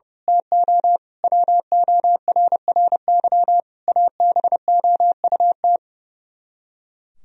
Morse code, heard as a keyed tone, about 30 wpm, 700 Hz. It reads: TO WORRY ABOUT
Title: Speed-Racing - Sets of 5 Words 30wpm